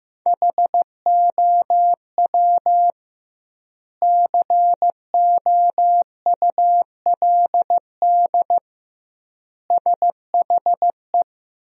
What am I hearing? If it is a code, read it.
Morse code, heard as a keyed tone, about 15 wpm, 700 Hz. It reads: HOW COULD SHE
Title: 2x Word-Spacing - Sentences from Top 200 Words - 15wpm